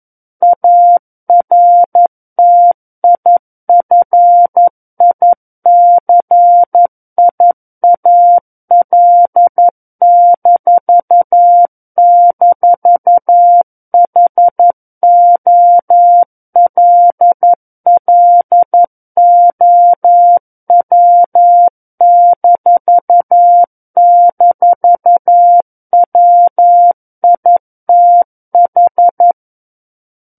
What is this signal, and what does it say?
Morse code, heard as a keyed tone, about 11 wpm, 700 Hz. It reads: ARTIFICIAL--HOLLOW--WITH
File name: Book One / 02-Chapter 2 - 11 WPM